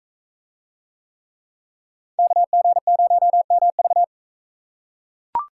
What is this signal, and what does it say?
Morse code, heard as a keyed tone, about 44 wpm, 700 Hz. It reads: KG0MV